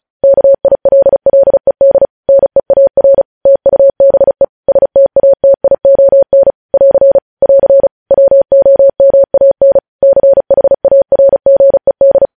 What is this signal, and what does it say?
Morse code, heard as a keyed tone, about 36 wpm, 550 Hz. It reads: KILLED NEAR TUBE STATION <AR> <AR> WOMAN CHARGED
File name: podcast / News headlines-1x-35x35